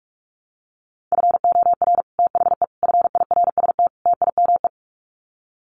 Morse code, heard as a keyed tone, about 44 wpm, 700 Hz. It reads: FOR THE FIRST TIME
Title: Sets of 4 Words 45wpm